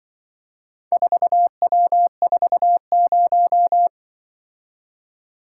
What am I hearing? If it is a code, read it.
Morse code, heard as a keyed tone, about 24 wpm, 700 Hz. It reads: 4W40
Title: Speed-Racing - Four Letter-Number Combinations - 20wpm